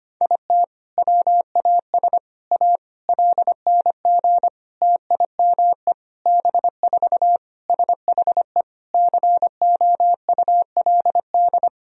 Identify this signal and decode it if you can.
Morse code, heard as a keyed tone, about 25 wpm, 700 Hz. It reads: IT WAS A LNG TIME B4 SHE COULD
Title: Sentences from Top 500 Words and Abbreviations 25wpm